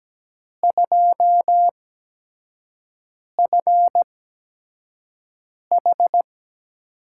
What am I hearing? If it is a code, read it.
Morse code, heard as a keyed tone, about 17 wpm, spaced as for 5 wpm, 700 Hz. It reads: 2FH
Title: USA Call Signs - 5 Characters - 8x Character Spacing - 17wpm